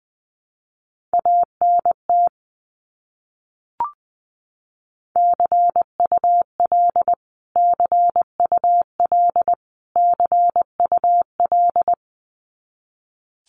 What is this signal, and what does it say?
Morse code, heard as a keyed tone, about 20 wpm, 700 Hz. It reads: ANT CUL CUL CUL
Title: Abbreviations Repeated 3x - 20wpm